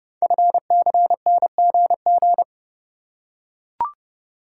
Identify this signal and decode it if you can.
Morse code, heard as a keyed tone, about 30 wpm, 700 Hz. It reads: FCNGG
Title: Code Groups - 5 Letters - 30wpm